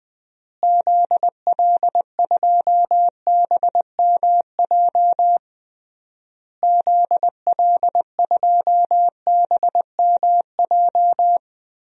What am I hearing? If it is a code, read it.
Morse code, heard as a keyed tone, about 20 wpm, 700 Hz. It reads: ZL2BMJ ZL2BMJ
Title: DXCC-100 Call Signs - 6 Characters - Repeated 3x - 20wpm